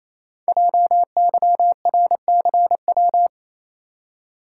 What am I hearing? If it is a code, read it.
Morse code, heard as a keyed tone, about 28 wpm, 700 Hz. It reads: JYRCW
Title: Code Groups - 5 Letter:Numbers - 28wpm